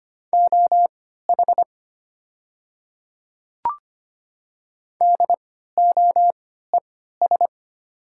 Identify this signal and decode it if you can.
Morse code, heard as a keyed tone, about 25 wpm, spaced as for 14 wpm, 690 Hz. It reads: OH DOES
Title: 3x Character Spacing - Top 500 Words - 25wpm